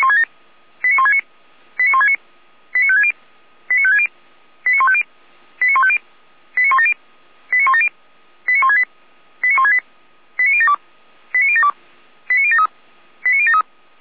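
Five-tone selective call (ZVEI), heard as quick runs of five tones, tones of about 70 ms. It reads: 89169 89169 89560 89560 89150 89150 89170 89170 89168 89168 89072 89072 89073 89073
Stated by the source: Five-tone ZVEI selective call received by SDR, from a server of Feuerwehr Orlamünde (fire brigade).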